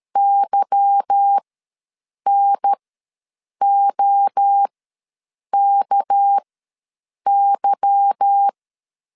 Morse code, heard as a keyed tone, about 13 wpm, spaced as for 7 wpm, 800 Hz. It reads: YNOKY
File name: LEZIONI / 8 YJ45 / 8 Y ws 12wpm 3409